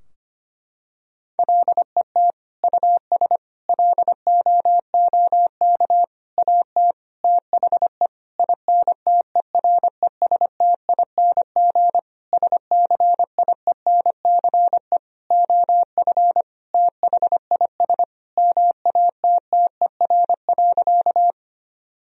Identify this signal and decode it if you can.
Morse code, heard as a keyed tone, about 25 wpm, 700 Hz. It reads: LET US LOOK AT THE INTERESTING SCIENCE OF THIS MATTER.
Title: Sentences from Top 1000 Words - Review Part 2 - 25wpm